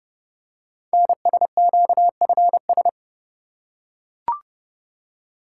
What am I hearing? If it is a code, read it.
Morse code, heard as a keyed tone, about 30 wpm, 700 Hz. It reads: NSQFS
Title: Code Groups - 5 Letters - 30wpm